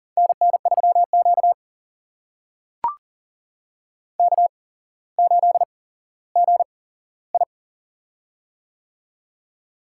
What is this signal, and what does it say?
Morse code, heard as a keyed tone, about 40 wpm, spaced as for 17 wpm, 700 Hz. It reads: NN3Q K8GI
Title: USA Call Signs - 4 Characters - 8x Character Spacing - 40wpm